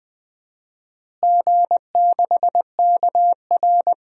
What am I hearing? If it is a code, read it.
Morse code, heard as a keyed tone, about 20 wpm, 700 Hz. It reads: G6KR
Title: DXCC-100 Call Signs - 4 Characters - Repeated 2x - 20wpm